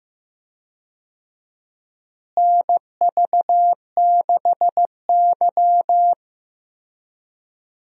Morse code, heard as a keyed tone, about 15 wpm, 700 Hz. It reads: NV6Y
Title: USA Call Signs - 4 Characters - Repeated 3x - 15wpm